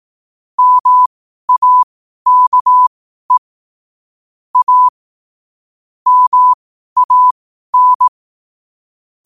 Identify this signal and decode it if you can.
Morse code, heard as a keyed tone, about 18 wpm, spaced as for 12 wpm, 1000 Hz. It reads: MAKE A MAN
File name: qotdfiles / 12wpm